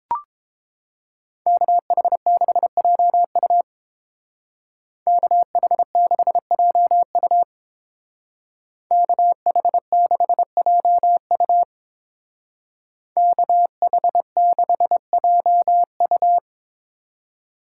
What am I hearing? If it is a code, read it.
Morse code, heard as a keyed tone, about 27 wpm, 700 Hz. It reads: KH6JU KH6JU KH6JU KH6JU
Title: Speed Racing - USA 2x2 Call Signs 22wpm